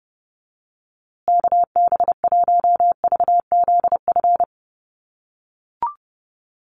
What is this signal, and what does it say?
Morse code, heard as a keyed tone, about 30 wpm, 700 Hz. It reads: KB1VZF
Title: USA Call Signs Repeated 2x 30wpm